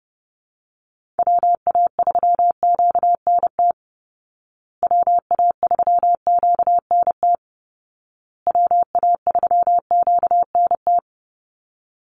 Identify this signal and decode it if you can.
Morse code, heard as a keyed tone, about 28 wpm, 700 Hz. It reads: WA3QNT WA3QNT WA3QNT
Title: Call Signs Repeated 2x - 30wpm